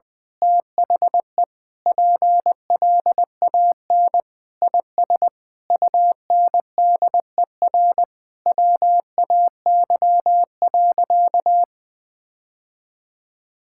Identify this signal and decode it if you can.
Morse code, heard as a keyed tone, about 21 wpm, 700 Hz. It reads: THE PLAN IS UNDER WAY.
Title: Sentences with 9-Word Phrases - 20wpm